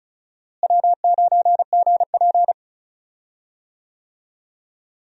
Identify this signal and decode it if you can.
Morse code, heard as a keyed tone, about 36 wpm, 700 Hz. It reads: W9GP
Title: USA Call Signs - 4 Characters - Repeated 2x - 35wpm